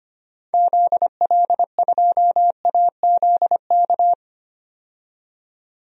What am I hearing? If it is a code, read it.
Morse code, heard as a keyed tone, about 25 wpm, 700 Hz. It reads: ZL2AZK
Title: DXCC-100 Call Signs - 6 Characters - 25wpm